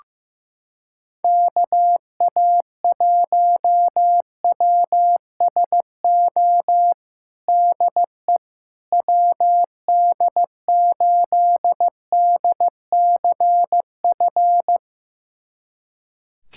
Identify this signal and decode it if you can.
Morse code, heard as a keyed tone, about 15 wpm, 700 Hz. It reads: KA1WSO DE WD8DCF
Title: USA Callsign DE Callsign - 6 Characters - 15wpm